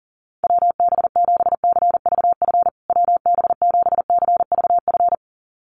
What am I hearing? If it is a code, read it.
Morse code, heard as a keyed tone, about 40 wpm, 700 Hz. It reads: WB7CVF WB7CVF
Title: USA Call Signs Repeated 2x 40wpm